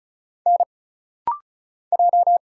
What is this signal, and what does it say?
Morse code, heard as a keyed tone, about 36 wpm, 700 Hz. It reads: N J
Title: Single Letter-Number - Rapid-Fire 35wpm